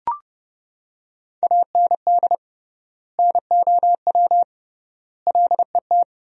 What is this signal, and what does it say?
Morse code, heard as a keyed tone, about 32 wpm, 700 Hz. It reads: AND NOW LET
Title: 3x Word-Spacing - Sentences from Top 300 Words - 30wpm